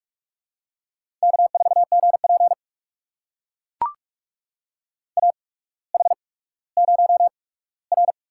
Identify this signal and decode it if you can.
Morse code, heard as a keyed tone, about 44 wpm, spaced as for 19 wpm, 700 Hz. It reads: K4GP AH0R
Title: USA Call Signs - 8x Character Spacing - 45wpm